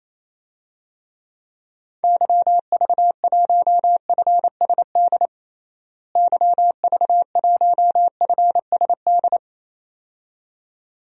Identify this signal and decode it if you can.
Morse code, heard as a keyed tone, about 27 wpm, 700 Hz. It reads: YV1FSD YV1FSD
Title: DXCC-100 Call Signs - 6 Characters - Repeated 2x - 28wpm